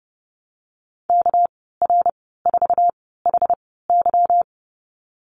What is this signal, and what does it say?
Morse code, heard as a keyed tone, about 30 wpm, spaced as for 17 wpm, 700 Hz. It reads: KR4HY
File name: Lesson 37g - Letter x - Call Signs 30wpm